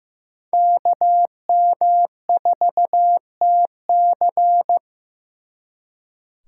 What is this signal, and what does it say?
Morse code, heard as a keyed tone, about 18 wpm, 700 Hz. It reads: KM4TC
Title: Speed Racing - USA 2x2 Call Signs 15wpm